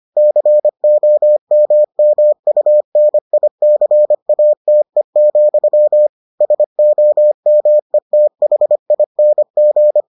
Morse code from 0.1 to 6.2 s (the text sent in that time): COMMUNICATE,